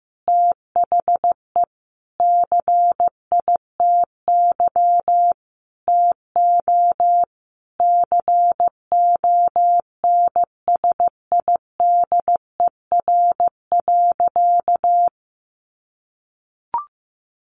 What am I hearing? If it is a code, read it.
Morse code, heard as a keyed tone, about 15 wpm, 700 Hz. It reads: THE CITY TO CONSIDER.
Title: Sentences from Top 900 Words 15wpm